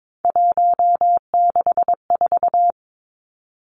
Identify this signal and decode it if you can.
Morse code, heard as a keyed tone, about 22 wpm, 700 Hz. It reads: 164
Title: Numbers - 3-Digit 22wpm